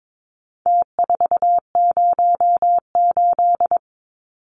Morse code, heard as a keyed tone, about 22 wpm, 700 Hz. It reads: T408
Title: Numbers - Contest Style - 4-Digit 22wpm